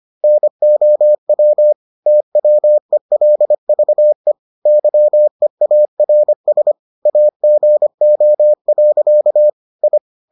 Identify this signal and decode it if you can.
Morse code, heard as a keyed tone, about 25 wpm, 600 Hz. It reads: NOW TWELVE YEARS AGO. I